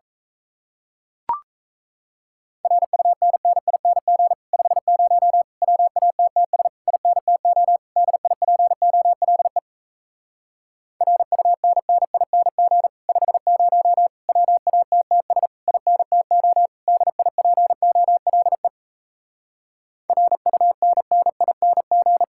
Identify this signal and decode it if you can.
Morse code, heard as a keyed tone, about 38 wpm, 700 Hz. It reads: RUNNING 50 WATTS INTO DIPOLE RUNNING 50 WATTS INTO DIPOLE RUNNING